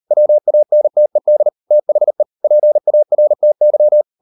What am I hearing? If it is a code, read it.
Morse code, heard as a keyed tone, about 38 wpm, 600 Hz. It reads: WANTED THE PARTY